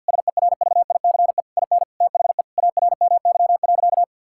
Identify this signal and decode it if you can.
Morse code, heard as a keyed tone, about 50 wpm, 700 Hz. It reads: SERVICE IN THE ARMY.
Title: Sentences from Top 1000 Words - Review Part 6 - 50wpm